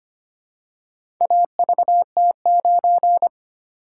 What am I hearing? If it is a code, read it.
Morse code, heard as a keyed tone, about 25 wpm, 700 Hz. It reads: AVT9
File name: practice / 25wpm / Four Letter-Number Combinations - 25wpm